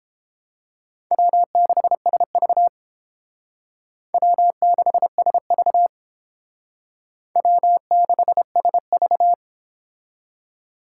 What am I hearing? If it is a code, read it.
Morse code, heard as a keyed tone, about 31 wpm, 700 Hz. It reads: W6SV W6SV W6SV